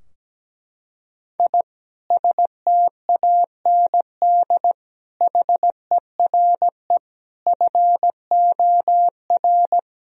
Morse code, heard as a keyed tone, about 17 wpm, 700 Hz. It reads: I STAND HERE FOR
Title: Sentences from Top 1000 Words - Review Part 3 - 17wpm